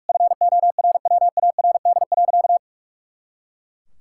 Morse code, heard as a keyed tone, about 41 wpm, 700 Hz. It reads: FORWARD.
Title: Pangram Challenge v2 - 45wpm